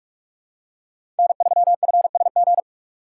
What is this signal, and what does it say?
Morse code, heard as a keyed tone, about 44 wpm, 690 Hz. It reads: N3FSG
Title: USA Call Signs - 5 Characters - 8x Character Spacing - 45wpm